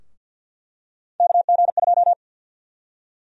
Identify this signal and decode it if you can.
Morse code, heard as a keyed tone, about 49 wpm, 700 Hz. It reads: KG2
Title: USA Call Sign Prefixes - Repeated 2x - 50wpm